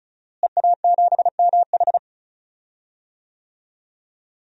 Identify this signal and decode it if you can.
Morse code, heard as a keyed tone, about 34 wpm, 700 Hz. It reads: EA7MH
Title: DXCC-100 Call Signs - 5 Characters - Repeated 2x - 35wpm